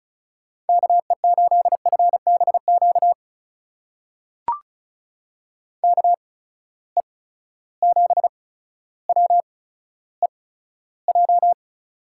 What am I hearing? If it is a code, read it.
Morse code, heard as a keyed tone, about 36 wpm, spaced as for 15 wpm, 700 Hz. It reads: KE8FBQ KE7WEJ
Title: USA Call Signs - 6 Characters - 8x Character Spacing - 35wpm